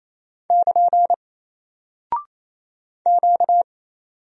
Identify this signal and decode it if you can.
Morse code, heard as a keyed tone, about 29 wpm, 700 Hz. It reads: <KN> Q
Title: Single Character (Letter:Number:Punctuation:Prosign) 28wpm